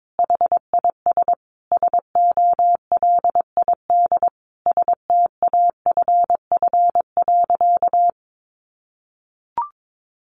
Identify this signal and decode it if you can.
Morse code, heard as a keyed tone, about 22 wpm, 700 Hz. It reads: HIS SOLID STAFF.
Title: Sentences from Top 1500 Words 22wpm